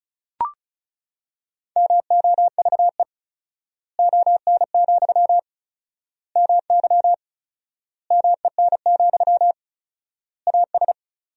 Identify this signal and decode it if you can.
Morse code, heard as a keyed tone, about 36 wpm, 700 Hz. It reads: MOVE ON, MY MEN, AS
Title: 4x Word-Spacing - Sentences from Top 500 Words - 35wpm